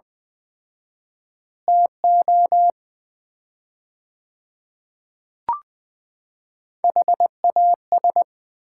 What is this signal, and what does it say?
Morse code, heard as a keyed tone, about 20 wpm, 700 Hz. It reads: TO HAS
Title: 4x Word-Spacing - Sets of 5 Words - 20wpm